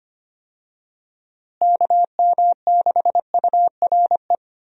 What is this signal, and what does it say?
Morse code, heard as a keyed tone, about 25 wpm, 700 Hz. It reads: KM6URE
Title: USA Call Signs - 6 Characters - Repeated 3x - 25wpm